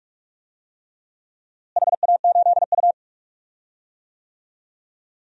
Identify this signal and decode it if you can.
Morse code, heard as a keyed tone, about 46 wpm, 690 Hz. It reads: HA8U
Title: DXCC-100 Call Signs - 4 Characters - Repeated 3x - 45wpm